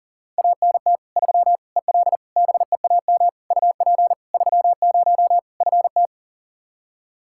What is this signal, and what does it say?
Morse code, heard as a keyed tone, about 40 wpm, 700 Hz. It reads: ANT 3 EL BEAM UP 30 FT
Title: Practice QSO - Exchange 2 - Info - 40wpm